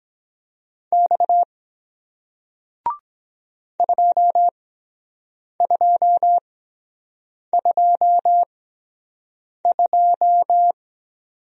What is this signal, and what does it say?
Morse code, heard as a keyed tone, about 23 wpm, spaced as for 7 wpm, 700 Hz. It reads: X 2222